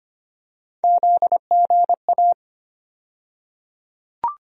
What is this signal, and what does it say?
Morse code, heard as a keyed tone, about 25 wpm, 700 Hz. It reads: ZGA
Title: USA Call Sign Postfixes - 3 Letters 25wpm